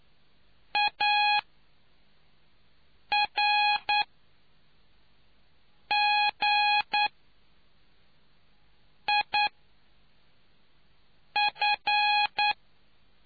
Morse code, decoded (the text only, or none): ARGIF